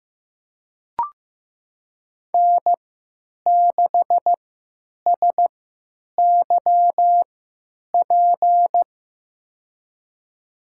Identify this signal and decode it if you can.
Morse code, heard as a keyed tone, about 15 wpm, spaced as for 9 wpm, 700 Hz. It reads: N6SYP
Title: USA Call Signs - 3x Character Spacing - 15wpm